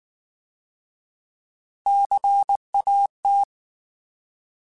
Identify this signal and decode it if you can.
Morse code, heard as a keyed tone, about 19 wpm, 800 Hz. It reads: CAT